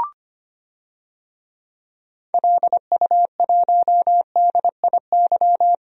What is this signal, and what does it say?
Morse code, heard as a keyed tone, about 25 wpm, 700 Hz. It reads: LU1DIY
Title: DXCC-100 Call Signs - 6 Characters - Repeated 3x - 25wpm